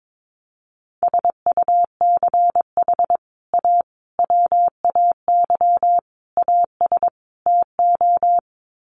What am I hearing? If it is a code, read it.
Morse code, heard as a keyed tone, about 22 wpm, 700 Hz. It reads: SUCH A WAY AS TO
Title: Sets of 5 Words Encore2 22wpm